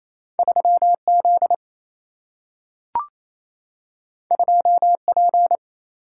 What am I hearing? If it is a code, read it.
Morse code, heard as a keyed tone, about 28 wpm, 700 Hz. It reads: 3 Z 2 P
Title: Single-Letter Single-Number 28wpm